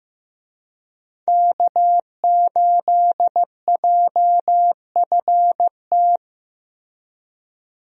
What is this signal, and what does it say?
Morse code, heard as a keyed tone, about 15 wpm, 700 Hz. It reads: K8JFT
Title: USA Call Signs - 5 Characters - Repeated 2x - 15wpm